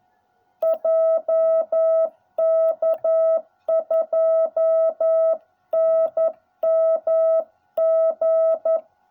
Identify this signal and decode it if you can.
Morse code, heard as a keyed tone, about 11 wpm, 660 Hz. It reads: JK2NMG